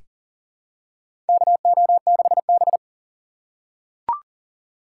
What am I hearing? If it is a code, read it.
Morse code, heard as a keyed tone, about 40 wpm, 700 Hz. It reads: KO6B
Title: USA Call Signs - 4x Character Spacing - 40wpm